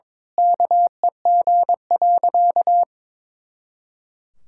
Morse code, heard as a keyed tone, about 21 wpm, 700 Hz. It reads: KEG.
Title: Pangram Challenge v2 - 22wpm